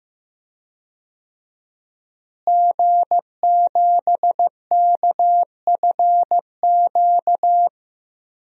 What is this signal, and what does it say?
Morse code, heard as a keyed tone, about 15 wpm, 700 Hz. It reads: G7KFQ
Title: DXCC-100 Call Signs - 5 Characters - Repeated 2x - 15wpm